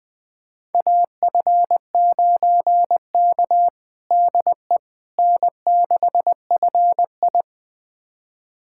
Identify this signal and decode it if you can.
Morse code, heard as a keyed tone, about 20 wpm, 700 Hz. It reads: AF9K DE N6FI
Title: USA Callsign DE Callsign - 4 Characters - 20wpm